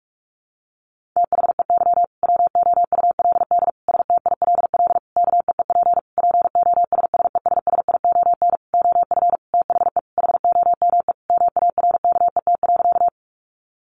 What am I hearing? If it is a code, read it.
Morse code, heard as a keyed tone, about 44 wpm, 700 Hz. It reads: THEY WOULD STILL KEEP POSSESSION OF THE HOME MARKET.
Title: Sentences from Top 1400 Words 45wpm